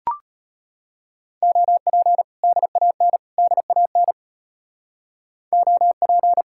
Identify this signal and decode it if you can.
Morse code, heard as a keyed tone, about 38 wpm, 700 Hz. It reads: OP DAN DAN OP